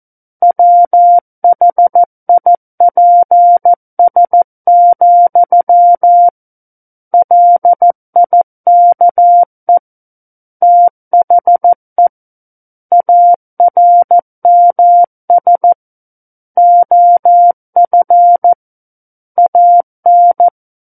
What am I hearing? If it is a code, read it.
Morse code, heard as a keyed tone, about 14 wpm, 700 Hz. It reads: WHIPS, LIKE THE ARMS OF AN
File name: Book One / 05-Chapter 5 - 14 WPM